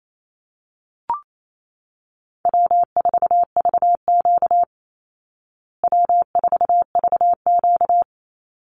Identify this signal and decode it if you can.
Morse code, heard as a keyed tone, about 28 wpm, 700 Hz. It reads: W4VQ W4VQ
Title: Easy Call Signs Repeated 3x - 28wpm